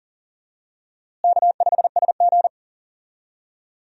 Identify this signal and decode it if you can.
Morse code, heard as a keyed tone, about 40 wpm, 700 Hz. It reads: K5SG
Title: USA Call Signs - 4 Characters - Repeated 3x - 40wpm